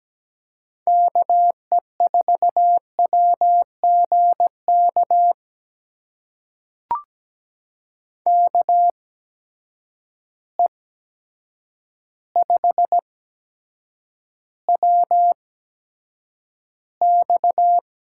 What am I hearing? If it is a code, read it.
Morse code, heard as a keyed tone, about 17 wpm, spaced as for 8 wpm, 700 Hz. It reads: KE4WGK KE5WX
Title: USA Call Signs - 6 Characters - 8x Character Spacing - 17wpm